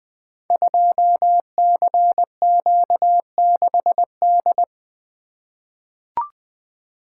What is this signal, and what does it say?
Morse code, heard as a keyed tone, about 20 wpm, 700 Hz. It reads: 2CQ6D
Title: Code Groups - 5 Letter:Numbers - 20wpm